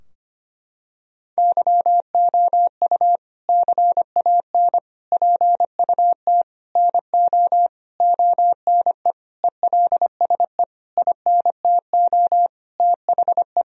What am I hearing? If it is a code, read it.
Morse code, heard as a keyed tone, about 25 wpm, 700 Hz. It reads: YOU CAN PUT NO ONE ELSE INTO THE
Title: Sentences from Top 1000 Words - Review Part 2 - 25wpm